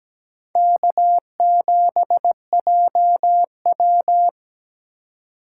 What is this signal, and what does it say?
Morse code, heard as a keyed tone, about 17 wpm, 700 Hz. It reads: K7JW
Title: USA Call Signs - 4 Characters - 2x Character Spacing - 17wpm